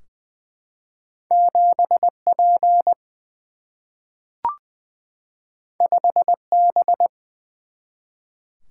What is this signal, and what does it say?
Morse code, heard as a keyed tone, about 20 wpm, 700 Hz. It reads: 7 P 5 B
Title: Single-Letter Single-Number 20wpm